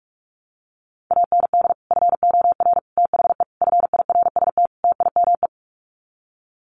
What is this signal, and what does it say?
Morse code, heard as a keyed tone, about 44 wpm, 700 Hz. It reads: AND FOR THE FIRST TIME
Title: Sets of 5 Words 45wpm